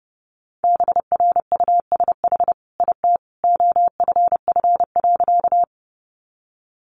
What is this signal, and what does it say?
Morse code, heard as a keyed tone, about 30 wpm, 700 Hz. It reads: BRUSH IT OFF.
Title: Sentences from Top 1100 Words 30wpm